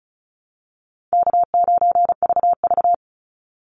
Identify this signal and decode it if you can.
Morse code, heard as a keyed tone, about 34 wpm, 700 Hz. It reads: K9VV
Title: Call Signs Repeated 2x - 35wpm